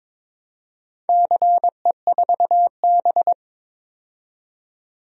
Morse code, heard as a keyed tone, about 21 wpm, 700 Hz. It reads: CE4B
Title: DXCC-100 Call Signs - 4 Characters - 22wpm